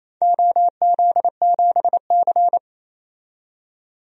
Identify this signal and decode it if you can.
Morse code, heard as a keyed tone, about 27 wpm, 700 Hz. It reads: OZ7C
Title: DXCC-100 Call Signs - 4 Characters - 28wpm